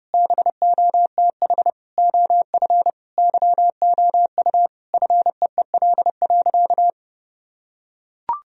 Morse code, heard as a keyed tone, about 32 wpm, 700 Hz. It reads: BOTH OF YOU FEEL.
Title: Sentences from Top 1000 Words - Review Part 1 - 30wpm